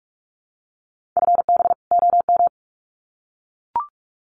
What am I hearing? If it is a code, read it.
Morse code, heard as a keyed tone, about 44 wpm, 700 Hz. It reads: FB OM
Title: Common QSO Phrases 45wpm